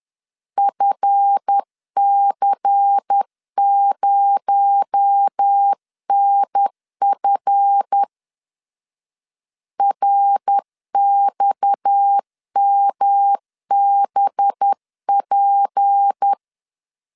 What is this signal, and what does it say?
Morse code, heard as a keyed tone, about 10 wpm, 800 Hz. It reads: FC0NF RXMBP
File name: LEZIONI / 7 GF9Z / 7 GF 10wpm 2514